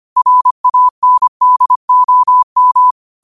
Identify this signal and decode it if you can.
Morse code, heard as a keyed tone, about 25 wpm, 1000 Hz. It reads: RANDOM
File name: qotdfiles / 25wpm